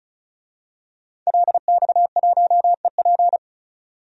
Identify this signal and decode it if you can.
Morse code, heard as a keyed tone, about 34 wpm, 700 Hz. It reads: LX1EP
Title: DXCC-100 Call Signs - 5 Characters - 35wpm